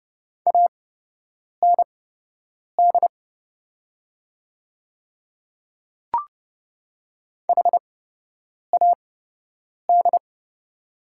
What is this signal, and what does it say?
Morse code, heard as a keyed tone, about 30 wpm, spaced as for 8 wpm, 700 Hz. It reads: AND HAD
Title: 8x Character Spacing - Top 100 Words - 30wpm